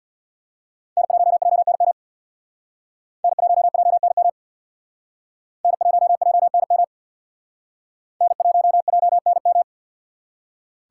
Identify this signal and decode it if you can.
Morse code, heard as a keyed tone, about 65 wpm, 700 Hz. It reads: N1JNM N1JNM N1JNM N1JNM